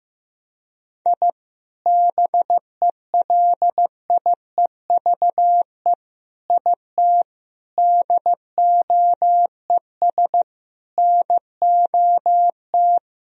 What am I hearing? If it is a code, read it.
Morse code, heard as a keyed tone, about 15 wpm, 700 Hz. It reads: I BELIEVE IT DOES NOT
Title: Sentences from Top 1000 Words - Review Part 2 - 15wpm